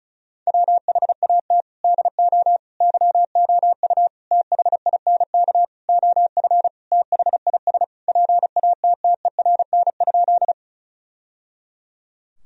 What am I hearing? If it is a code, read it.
Morse code, heard as a keyed tone, about 34 wpm, 700 Hz. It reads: WHAT DO YOU THINK OF THIS PATTERN?
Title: Sentences from Top 1000 Words - Review Part 2 - 35wpm